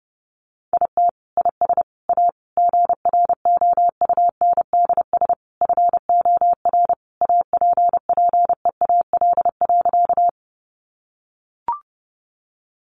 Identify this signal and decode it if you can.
Morse code, heard as a keyed tone, about 28 wpm, 700 Hz. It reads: IT IS A GROUNDS FOR APPEAL.
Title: Sentences from Top 1300 Words 30wpm